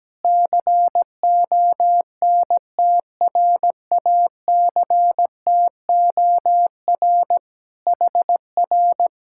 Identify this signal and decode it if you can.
Morse code, heard as a keyed tone, about 17 wpm, 700 Hz. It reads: CONTRACTOR HR
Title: Speed-Racing Common QSO Phrases - 17wpm